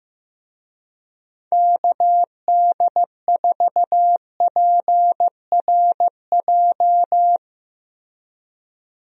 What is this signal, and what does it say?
Morse code, heard as a keyed tone, about 16 wpm, 700 Hz. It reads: KD4PRJ
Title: USA Call Signs - 6 Characters - Repeated 2x - 15wpm